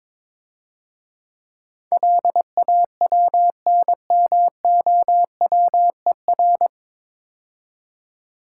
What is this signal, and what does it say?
Morse code, heard as a keyed tone, about 22 wpm, 690 Hz. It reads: LAWNMOWER
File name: Easy Words - Speed-Racing with Hints - 22wpm